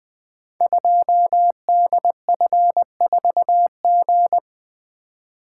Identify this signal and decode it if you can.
Morse code, heard as a keyed tone, about 20 wpm, 700 Hz. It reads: 2DF4G
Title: Code Groups - 5 Letter:Numbers - 20wpm